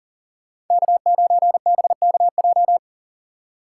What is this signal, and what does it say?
Morse code, heard as a keyed tone, about 52 wpm, 700 Hz. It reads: K9BKJ